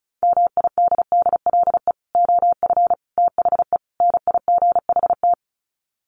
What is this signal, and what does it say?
Morse code, heard as a keyed tone, about 34 wpm, 700 Hz. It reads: MIDDLE OF THE NIGHT
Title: Sets of 4 Words 35wpm